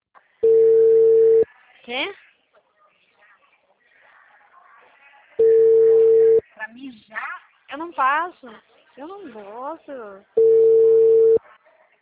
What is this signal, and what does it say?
Ringback tone (China, 450 Hz), with speech.